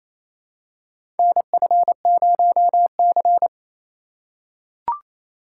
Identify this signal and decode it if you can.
Morse code, heard as a keyed tone, about 28 wpm, 700 Hz. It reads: NF0C
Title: USA Call Signs - 4 Characters - 2x Character Spacing - 28wpm